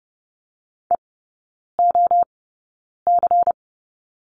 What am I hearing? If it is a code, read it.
Morse code, heard as a keyed tone, about 30 wpm, spaced as for 9 wpm, 690 Hz. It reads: EOC
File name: Lesson 16e - Letter c - Three Characters 30wpm